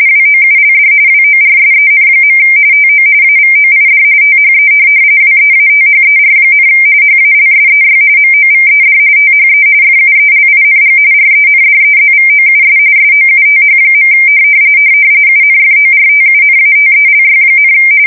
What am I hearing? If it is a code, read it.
RTTY, about 45.45 baud. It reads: WOULD WORSEN. TO AVOID THE WEATHER, HALSEY MOVED THE FUELING RENDEZVOUS SOUTH. UNFORTUNATELY,
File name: fsk170 / USS-New-Jersey